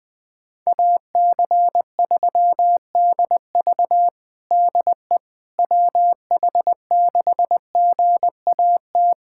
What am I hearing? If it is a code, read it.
Morse code, heard as a keyed tone, about 21 wpm, 700 Hz. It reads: AC3DV DE WH6GAT
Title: USA Callsign DE Callsign 20wpm